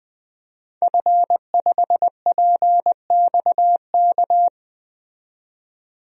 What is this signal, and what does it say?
Morse code, heard as a keyed tone, about 20 wpm, 700 Hz. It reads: F5PXK
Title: DXCC-100 Call Signs - 5 Characters - 20wpm